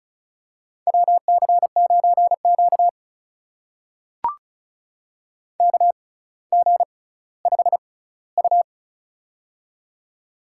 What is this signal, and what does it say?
Morse code, heard as a keyed tone, about 36 wpm, spaced as for 18 wpm, 700 Hz. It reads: WC9Q KG5U
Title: USA Call Signs - 6x Character Spacing - 35wpm